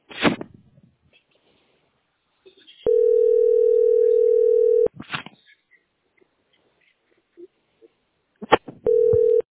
Ringback tone (North America), 440 + 480 Hz.